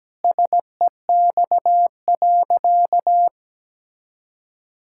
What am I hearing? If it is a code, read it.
Morse code, heard as a keyed tone, about 17 wpm, 700 Hz. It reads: SEX.
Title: Sentences from Top 1600 Words 17wpm